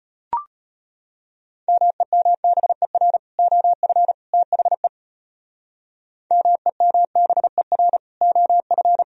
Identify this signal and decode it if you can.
Morse code, heard as a keyed tone, about 33 wpm, 700 Hz. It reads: MEMBER OF THE MEMBER OF